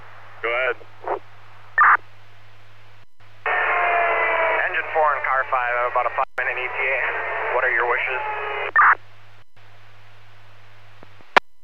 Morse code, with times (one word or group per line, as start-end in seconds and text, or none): none